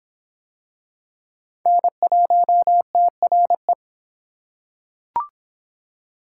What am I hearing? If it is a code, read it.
Morse code, heard as a keyed tone, about 23 wpm, 700 Hz. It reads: N1TRE